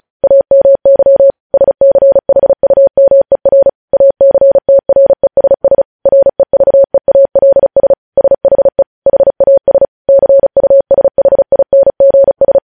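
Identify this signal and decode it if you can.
Morse code, heard as a keyed tone, about 36 wpm, 550 Hz. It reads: AMY SCHUMER ACTRESS REVEALS SHE HAS CUSHINGS
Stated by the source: Morse practice sending news headlines at 35 wpm.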